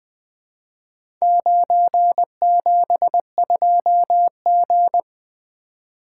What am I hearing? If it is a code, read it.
Morse code, heard as a keyed tone, about 20 wpm, 700 Hz. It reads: 972G
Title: Four Letter-Number Combinations - 20wpm